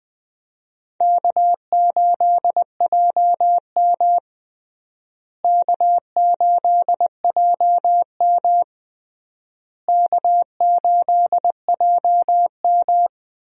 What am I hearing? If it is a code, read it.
Morse code, heard as a keyed tone, about 21 wpm, 700 Hz. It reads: K8JM K8JM K8JM
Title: USA Call Signs - 4 Characters - Repeated 3x - 20wpm